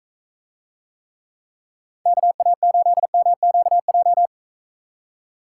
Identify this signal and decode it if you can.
Morse code, heard as a keyed tone, about 34 wpm, 700 Hz. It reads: KA8MQJ
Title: Speed Racing - USA 2x3 Call Signs 28wpm